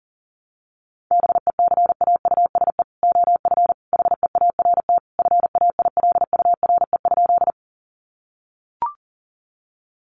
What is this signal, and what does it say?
Morse code, heard as a keyed tone, about 40 wpm, 700 Hz. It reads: BECAUSE OF HEART FAILURE?
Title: Sentences from Top 1200 Words 40wpm